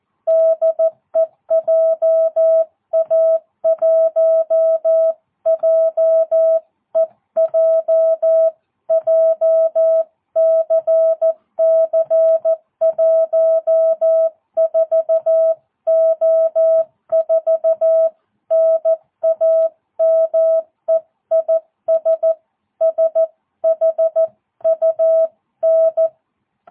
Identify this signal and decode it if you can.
Morse code, heard as a keyed tone, about 14 wpm, 660 Hz. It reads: DE JA1JEJ JCC14O4 NAME IS SHUN
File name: jh8cbh / 2022cw